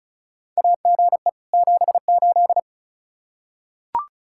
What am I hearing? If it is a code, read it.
Morse code, heard as a keyed tone, about 36 wpm, 700 Hz. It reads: AGE 78
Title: Practice QSO - Exchange 3 - More Info - 35wpm